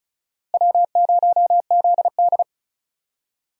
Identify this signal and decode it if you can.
Morse code, heard as a keyed tone, about 36 wpm, 700 Hz. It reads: W0ZD
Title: USA Call Signs - 4 Characters - Repeated 2x - 35wpm